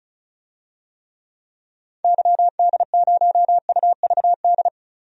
Morse code, heard as a keyed tone, about 34 wpm, 700 Hz. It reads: YD0UVD
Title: DXCC-100 Call Signs - 6 Characters - 35wpm